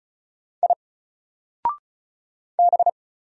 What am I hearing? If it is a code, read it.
Morse code, heard as a keyed tone, about 36 wpm, 700 Hz. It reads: I B
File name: Single-Letter-Number-35wpm